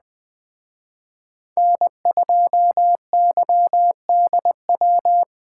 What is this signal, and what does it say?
Morse code, heard as a keyed tone, about 20 wpm, 700 Hz. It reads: N2YDW